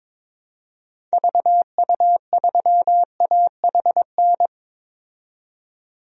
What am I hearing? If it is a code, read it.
Morse code, heard as a keyed tone, about 22 wpm, 700 Hz. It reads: VU3AHN
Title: DXCC-100 Call Signs - 6 Characters - 22wpm